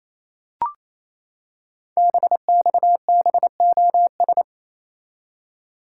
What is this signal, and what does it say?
Morse code, heard as a keyed tone, about 27 wpm, 700 Hz. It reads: BXBOS
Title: Code Groups - 5 Letters - 28wpm